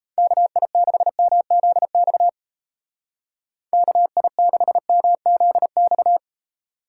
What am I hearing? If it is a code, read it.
Morse code, heard as a keyed tone, about 35 wpm, 700 Hz. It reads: KI6MZX KI6MZX